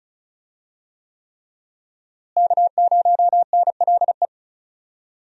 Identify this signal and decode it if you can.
Morse code, heard as a keyed tone, about 36 wpm, 700 Hz. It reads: K0NLE
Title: USA Call Signs - 5 Characters - Repeated 3x - 35wpm